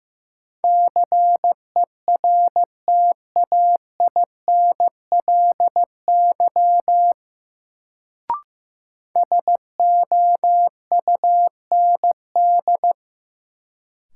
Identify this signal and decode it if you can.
Morse code, heard as a keyed tone, about 15 wpm, 700 Hz. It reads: CERTAINLY SOUND
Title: Sentences from Top 900 Words - New Word Drill - 15wpm